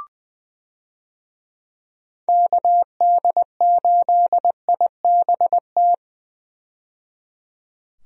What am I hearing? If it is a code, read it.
Morse code, heard as a keyed tone, about 21 wpm, 700 Hz. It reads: KD8IBT
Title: USA Call Signs - 6 Characters - 20wpm